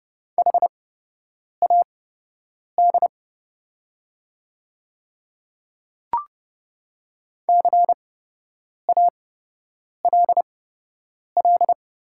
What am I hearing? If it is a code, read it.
Morse code, heard as a keyed tone, about 30 wpm, spaced as for 8 wpm, 700 Hz. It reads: HAD CALL
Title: 8x Character Spacing - Top 100 Words - 30wpm